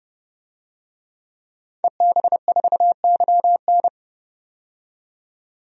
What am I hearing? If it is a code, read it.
Morse code, heard as a keyed tone, about 31 wpm, 700 Hz. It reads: EB4YN
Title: DXCC-100 Call Signs - 5 Characters - 30wpm